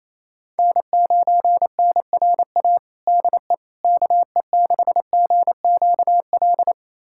Morse code, heard as a keyed tone, about 28 wpm, 700 Hz. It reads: N9NRA DE KE6GQL
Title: USA Callsign DE Callsign 28wpm